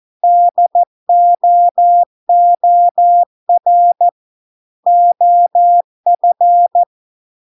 Morse code, heard as a keyed tone, about 14 wpm, 700 Hz. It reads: DOOR OF